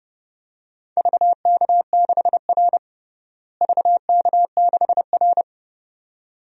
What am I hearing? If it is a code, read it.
Morse code, heard as a keyed tone, about 32 wpm, 700 Hz. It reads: VK6R VK6R
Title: DXCC-100 Call Signs - 4 Characters - Repeated 2x - 30wpm